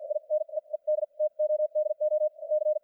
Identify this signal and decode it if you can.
Morse code, heard as a keyed tone, about 54 wpm, spaced as for 41 wpm, 610 Hz, with radio noise. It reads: INE ED TODO C
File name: archived / CRSnet